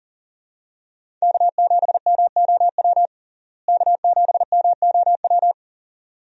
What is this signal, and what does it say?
Morse code, heard as a keyed tone, about 40 wpm, 700 Hz. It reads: K7MOW K7MOW
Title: USA Call Signs - 5 Characters - Repeated 2x - 40wpm